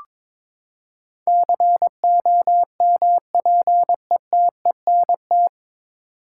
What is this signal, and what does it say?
Morse code, heard as a keyed tone, about 22 wpm, 700 Hz. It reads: COMPETENT
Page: Morse Code Ninja practice 22wpm